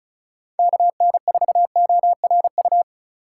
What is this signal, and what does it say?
Morse code, heard as a keyed tone, about 36 wpm, 700 Hz. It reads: KN4ORU
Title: USA Call Signs - 6 Characters - 2x Character Spacing - 35wpm